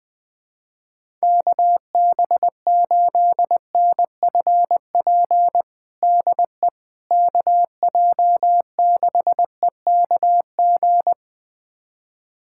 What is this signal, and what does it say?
Morse code, heard as a keyed tone, about 21 wpm, 700 Hz. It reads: KB8NFP DE KJ6EKG
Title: USA Callsign DE Callsign - 6 Characters - 20wpm